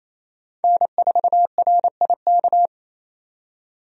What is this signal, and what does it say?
Morse code, heard as a keyed tone, about 28 wpm, 700 Hz. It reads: N4RIK
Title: USA Call Signs - 5 Characters - 28wpm